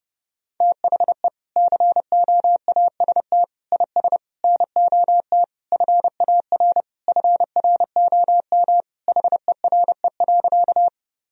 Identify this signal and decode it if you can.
Morse code, heard as a keyed tone, about 30 wpm, 700 Hz. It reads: THE COAST IS NOT FAR FROM HERE.
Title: Sentences from Top 1000 Words - Review Part 4 - 30wpm